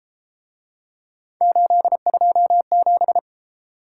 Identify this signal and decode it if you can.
Morse code, heard as a keyed tone, about 33 wpm, 700 Hz. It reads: 827